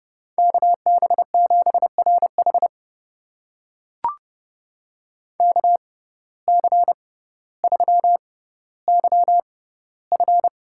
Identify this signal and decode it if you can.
Morse code, heard as a keyed tone, about 30 wpm, spaced as for 15 wpm, 700 Hz. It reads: KB7RH KC3YF
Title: USA Call Signs - 5 Characters - 6x Character Spacing - 30wpm